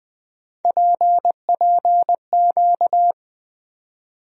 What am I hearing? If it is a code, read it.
Morse code, heard as a keyed tone, about 20 wpm, 700 Hz. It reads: PPQ
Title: USA Call Sign Postfixes - 3 Letters 20wpm